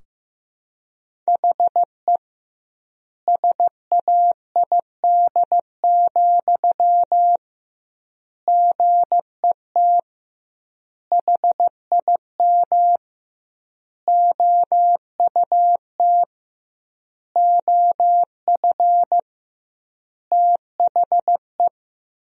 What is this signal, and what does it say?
Morse code, heard as a keyed tone, about 15 wpm, 700 Hz. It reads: HE SAID, GET HIM OUT OF THE
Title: 2x Word-Spacing - Sentences from Top 400 Words - 15wpm